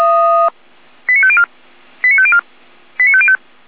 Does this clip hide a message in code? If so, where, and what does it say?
Yes: five-tone selective call (ZVEI) from 1.1 to 3.4 s, heard as quick runs of five tones, tones of about 70 ms. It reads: 89504 89504 89505